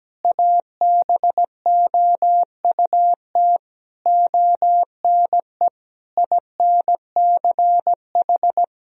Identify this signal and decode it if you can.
Morse code, heard as a keyed tone, about 17 wpm, 700 Hz. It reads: ABOUT ONE INCH
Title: Sentences from Top 1000 Words - Review Part 3 - 17wpm